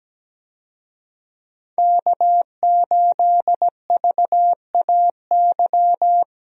Morse code, heard as a keyed tone, about 17 wpm, 700 Hz. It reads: K8VAY
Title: USA Call Signs - 5 Characters - 17wpm